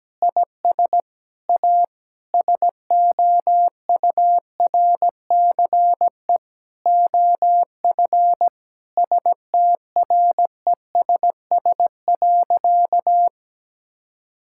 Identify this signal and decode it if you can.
Morse code, heard as a keyed tone, about 17 wpm, 700 Hz. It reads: IS A SOURCE OF STRESS.
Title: Sentences from Top 1000 Words - Review Part 6 - 17wpm